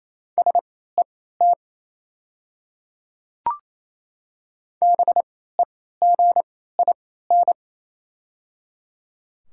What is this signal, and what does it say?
Morse code, heard as a keyed tone, about 28 wpm, spaced as for 16 wpm, 700 Hz. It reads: SET BEGIN
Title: 3x Character Spacing - Top 300 Words - 28wpm